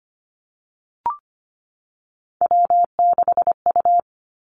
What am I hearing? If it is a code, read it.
Morse code, heard as a keyed tone, about 25 wpm, 700 Hz. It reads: W6U
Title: Three Challenging Letter-Number Combinations 25wpm